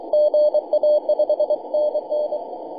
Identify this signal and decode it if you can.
Morse code, heard as a keyed tone, about 23 wpm, 590 Hz, with radio noise. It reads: GA 5NN
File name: posts / may-04-0625z-je2ehp-jd1-10-122mhz-cw